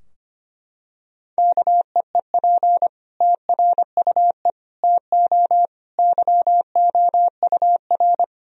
Morse code, heard as a keyed tone, about 25 wpm, 700 Hz. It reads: KEEP TRUE TO YOUR
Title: Sentences from Top 1000 Words - Review Part 3 - 25wpm